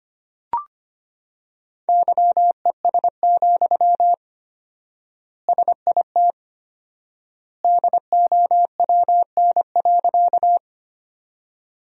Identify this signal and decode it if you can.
Morse code, heard as a keyed tone, about 25 wpm, 700 Hz. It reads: YES, SIT DOWN.
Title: 4x Word-Spacing - Sentences from Top 500 Words - 25wpm